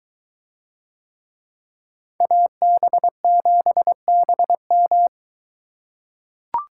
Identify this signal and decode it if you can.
Morse code, heard as a keyed tone, about 22 wpm, 700 Hz. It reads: AB7BM